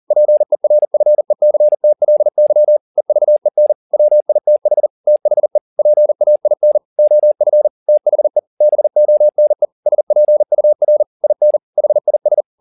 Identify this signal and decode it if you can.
Morse code, heard as a keyed tone, about 40 wpm, 600 Hz. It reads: PERFECTLY EVEN WITH THE PAIN OF THE BONE SPUR IN HIS